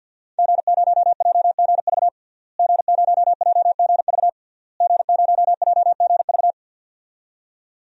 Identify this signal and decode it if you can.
Morse code, heard as a keyed tone, about 50 wpm, 700 Hz. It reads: G0JGV G0JGV G0JGV
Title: DXCC-100 Call Signs - 5 Characters - Repeated 3x - 50wpm